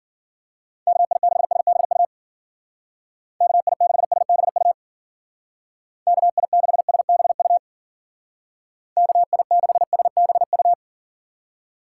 Morse code, heard as a keyed tone, about 51 wpm, 700 Hz. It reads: KI6SBU KI6SBU KI6SBU KI6SBU